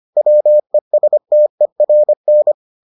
Morse code, heard as a keyed tone, about 25 wpm, 600 Hz. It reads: WESTERN